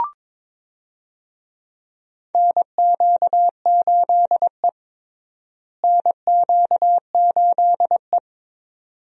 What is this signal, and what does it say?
Morse code, heard as a keyed tone, about 22 wpm, 700 Hz. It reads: NQ8E NQ8E
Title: USA Call Signs - 4 Characters - Repeated 2x - 22wpm